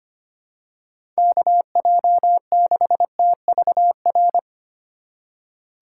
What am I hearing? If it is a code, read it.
Morse code, heard as a keyed tone, about 25 wpm, 700 Hz. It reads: KJ6TVR
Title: USA Call Signs - 6 Characters - 4x Character Spacing - 25wpm